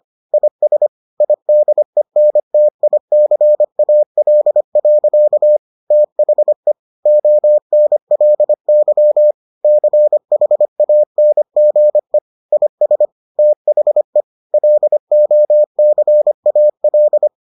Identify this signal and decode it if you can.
Morse code, heard as a keyed tone, about 25 wpm, 600 Hz. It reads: IS IDENTICAL. THE ONLY CHANGE IS THE LOCAL